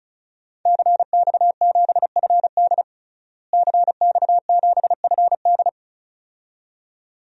Morse code, heard as a keyed tone, about 36 wpm, 700 Hz. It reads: CX7FD CX7FD
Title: DXCC-100 Call Signs - 5 Characters - Repeated 2x - 35wpm